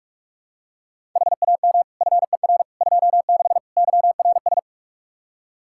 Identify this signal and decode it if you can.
Morse code, heard as a keyed tone, about 46 wpm, 700 Hz. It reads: HAM FER 26 YRS
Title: Practice QSO - Exchange 3 - More Info - 45wpm